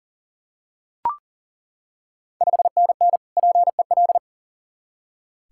Morse code, heard as a keyed tone, about 40 wpm, 690 Hz. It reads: 5NN PEL
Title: Ontario QSO Party Exchange 40wpm